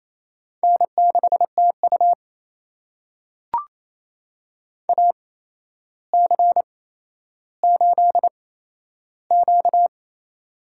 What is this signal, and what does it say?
Morse code, heard as a keyed tone, about 27 wpm, spaced as for 12 wpm, 700 Hz. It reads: N6TU AC8Q
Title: USA Call Signs - 4 Characters - 8x Character Spacing - 28wpm